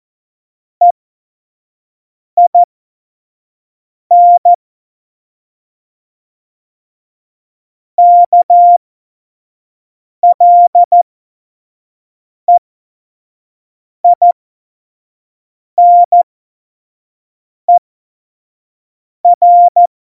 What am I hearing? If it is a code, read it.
Morse code, heard as a keyed tone, about 14 wpm, spaced as for 5 wpm, 700 Hz. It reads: EIN KLEINER
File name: Morsekurs / Witze / Witz 14 April 2020, 25 F70